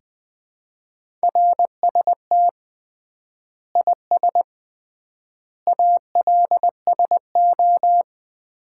Morse code, heard as a keyed tone, about 21 wpm, 700 Hz. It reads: RST IS ALSO
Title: Common QSO Phrases - 3x Word-Spacing - 20wpm